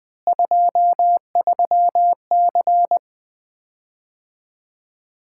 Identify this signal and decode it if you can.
Morse code, heard as a keyed tone, about 20 wpm, 700 Hz. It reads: 23C
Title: Common QSO Phrases - 4x Word-Spacing - 20wpm